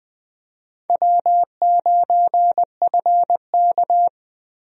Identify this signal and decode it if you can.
Morse code, heard as a keyed tone, about 20 wpm, 700 Hz. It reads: W9FK
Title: USA Call Signs - 4 Characters - Repeated 2x - 20wpm